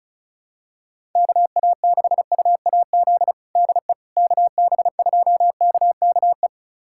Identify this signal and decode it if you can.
Morse code, heard as a keyed tone, about 36 wpm, 700 Hz. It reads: KA6UAZ DE KB2KKE
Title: USA Callsign DE Callsign - 6 Characters - 35wpm